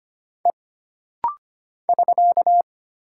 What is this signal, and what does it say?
Morse code, heard as a keyed tone, about 25 wpm, 700 Hz. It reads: E <SK>
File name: Single Character (Letter:Number:Punctuation:Prosign) - Rapid-Fire - 25wpm